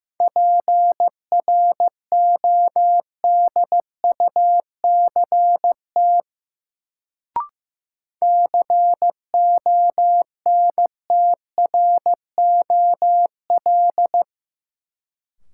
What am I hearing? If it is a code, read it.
Morse code, heard as a keyed tone, about 15 wpm, 700 Hz. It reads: PRODUCT CONTROL
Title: Sentences from Top 1000 Words - Review Part 2 - New Word Drill - 15wpm